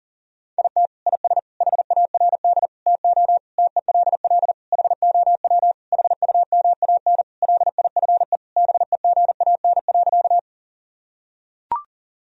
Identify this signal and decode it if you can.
Morse code, heard as a keyed tone, about 40 wpm, 700 Hz. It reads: IT IS HARD TO TELL HOW HUMAN LIFE BEGAN.
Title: Sentences from Top 1000 Words - Review Part 3 - 40wpm